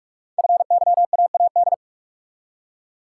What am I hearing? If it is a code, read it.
Morse code, heard as a keyed tone, about 44 wpm, 700 Hz. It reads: FYAAD
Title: Code Groups - 5 Letters - 45wpm